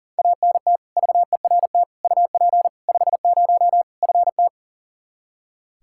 Morse code, heard as a keyed tone, about 37 wpm, 700 Hz. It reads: ANT VERT UP 50 FT
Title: Practice QSO - Exchange 2 - Info - 40wpm